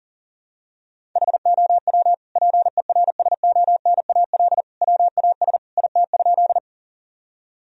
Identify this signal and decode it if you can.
Morse code, heard as a keyed tone, about 37 wpm, 700 Hz. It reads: HOW PERSONAL WAS IT?
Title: Sentences from Top 1000 Words - Review Part 4 - 40wpm